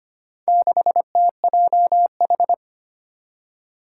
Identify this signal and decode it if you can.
Morse code, heard as a keyed tone, about 25 wpm, 700 Hz. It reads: 6TJH
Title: Four Challenging Letter-Number Combinations 25wpm